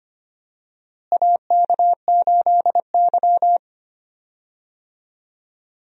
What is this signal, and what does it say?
Morse code, heard as a keyed tone, about 25 wpm, 700 Hz. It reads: AK8Y
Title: USA Call Signs - 4 Characters - Repeated 2x - 25wpm